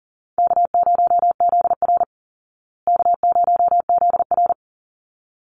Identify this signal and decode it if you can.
Morse code, heard as a keyed tone, about 40 wpm, 700 Hz. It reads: K0ZR K0ZR
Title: Easy Call Signs Repeated 2x - 40wpm